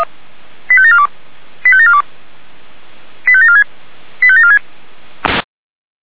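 Five-tone selective call (ZVEI), heard as quick runs of five tones, tones of about 70 ms. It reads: 85742 85742 85746 85746
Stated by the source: ZVEI five-tone selective call 85753 received via SDR.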